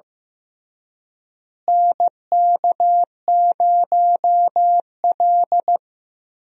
Morse code, heard as a keyed tone, about 15 wpm, 700 Hz. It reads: NK0L
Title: USA Call Signs - 4 Characters - Repeated 3x - 15wpm